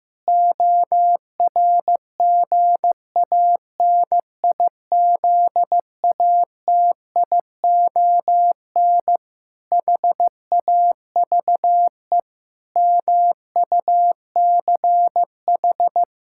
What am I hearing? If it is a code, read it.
Morse code, heard as a keyed tone, about 14 wpm, 700 Hz. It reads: ORGANIZATION HAVE MUCH
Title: Sentences from Top 1000 Words - Review Part 5 - 15wpm